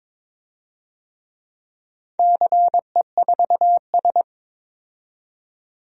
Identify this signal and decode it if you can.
Morse code, heard as a keyed tone, about 22 wpm, 700 Hz. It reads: CE4S
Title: DXCC-100 Call Signs - 4 Characters - 22wpm